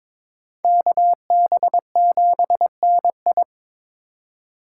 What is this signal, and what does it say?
Morse code, heard as a keyed tone, about 26 wpm, 700 Hz. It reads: KB7NI
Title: Speed Racing - USA 2x2 Call Signs 22wpm